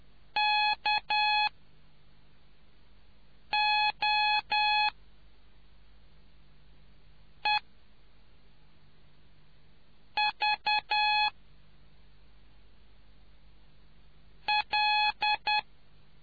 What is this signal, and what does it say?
Morse code, heard as a keyed tone, about 10 wpm, spaced as for 3 wpm, 840 Hz. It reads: KOEVL